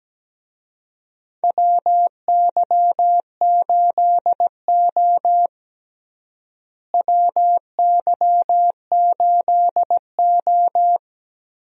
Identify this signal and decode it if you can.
Morse code, heard as a keyed tone, about 17 wpm, 700 Hz. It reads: WY8O WY8O
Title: USA Call Signs - 4 Characters - Repeated 3x - 17wpm